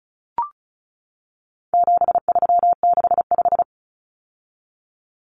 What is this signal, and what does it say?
Morse code, heard as a keyed tone, about 34 wpm, 700 Hz. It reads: 7365
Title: Common Cut Numbers - 4-Digit 35wpm